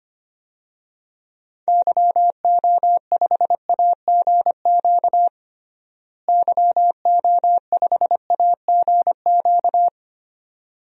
Morse code, heard as a keyed tone, about 25 wpm, 700 Hz. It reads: YO5AGQ YO5AGQ
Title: DXCC-100 Call Signs - 6 Characters - Repeated 2x - 25wpm